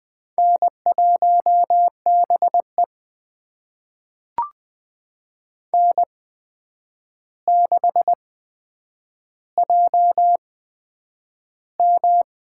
Morse code, heard as a keyed tone, about 20 wpm, spaced as for 9 wpm, 700 Hz. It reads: N1BE N6JM
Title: USA Call Signs - 4 Characters - 8x Character Spacing - 20wpm